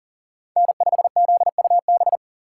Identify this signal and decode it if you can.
Morse code, heard as a keyed tone, about 40 wpm, 690 Hz. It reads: N5ZUB
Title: USA Call Signs - 6x Character Spacing - 40wpm